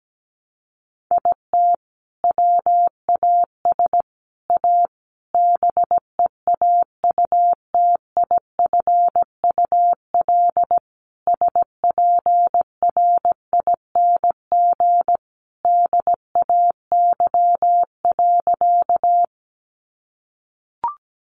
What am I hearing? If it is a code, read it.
Morse code, heard as a keyed tone, about 17 wpm, 700 Hz. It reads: IT WAS A BEAUTIFUL SPRING DAY.
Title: Sentences from Top 1200 Words 17wpm